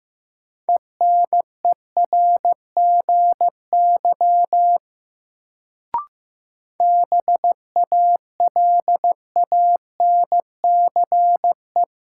Morse code, heard as a keyed tone, about 15 wpm, 700 Hz. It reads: ENERGY BALANCE
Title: Sentences from Top 900 Words - New Word Drill - 15wpm